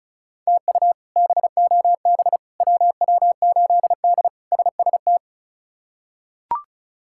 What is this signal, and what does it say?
Morse code, heard as a keyed tone, about 36 wpm, 700 Hz. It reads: TU BOB WW8D SST
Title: K1USN SST QSO Practice - 35wpm